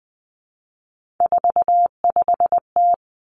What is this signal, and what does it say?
Morse code, heard as a keyed tone, about 20 wpm, 700 Hz. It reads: 45T
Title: Three Challenging Letter-Number Combinations 20wpm